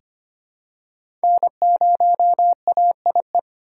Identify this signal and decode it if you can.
Morse code, heard as a keyed tone, about 25 wpm, 700 Hz. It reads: N0AIE
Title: K1USN SST QSO Practice - 25wpm